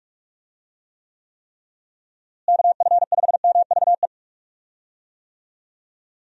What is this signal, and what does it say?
Morse code, heard as a keyed tone, about 46 wpm, 700 Hz. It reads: KF5MVE
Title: USA Call Signs - 6 Characters - Repeated 3x - 45wpm